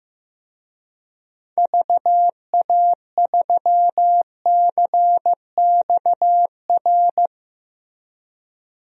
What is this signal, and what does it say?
Morse code, heard as a keyed tone, about 15 wpm, 700 Hz. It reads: VA3CXR
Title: DXCC-100 Call Signs - 6 Characters - 15wpm